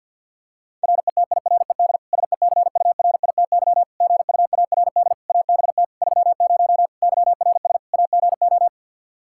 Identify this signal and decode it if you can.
Morse code, heard as a keyed tone, about 50 wpm, 700 Hz. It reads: RETIRED SECURITY GUARD ABT 30 YRS AGO
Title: K1USN SST QSO Practice 50wpm